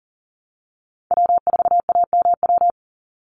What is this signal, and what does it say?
Morse code, heard as a keyed tone, about 41 wpm, 700 Hz. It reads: W4AMW
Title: NA SOTA Chaser Callsigns 40wpm